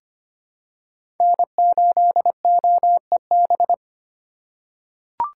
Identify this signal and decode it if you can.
Morse code, heard as a keyed tone, about 25 wpm, 700 Hz. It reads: N8OEB
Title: USA Call Signs - 4x Character Spacing - 25wpm